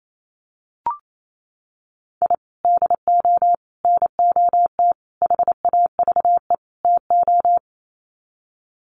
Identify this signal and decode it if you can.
Morse code, heard as a keyed tone, about 28 wpm, 700 Hz. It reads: I DO NOT HAVE TO
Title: Sets of 5 Words 28wpm